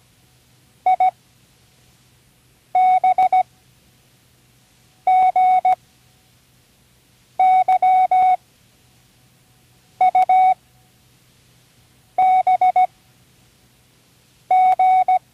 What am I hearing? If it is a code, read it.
Morse code, heard as a keyed tone, about 17 wpm, spaced as for 5 wpm, 740 Hz. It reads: IBGYUBG
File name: Learning Morse Code (from Tune in the World)